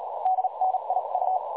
Morse code, heard as a keyed tone, about 38 wpm, spaced as for 32 wpm, 750 Hz, with noise. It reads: DNIU
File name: dlazaawansowanych / nagraniazpasma / websdr recording 3565.0kHz